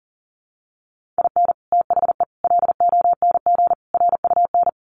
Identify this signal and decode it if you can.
Morse code, heard as a keyed tone, about 40 wpm, 700 Hz. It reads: IN THE LONG RUN
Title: Sets of 4 Words 40wpm